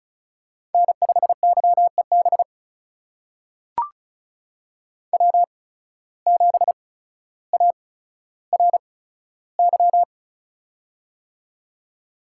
Morse code, heard as a keyed tone, about 36 wpm, spaced as for 15 wpm, 700 Hz. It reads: N5YEB W7ARY
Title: USA Call Signs - 5 Characters - 8x Character Spacing - 35wpm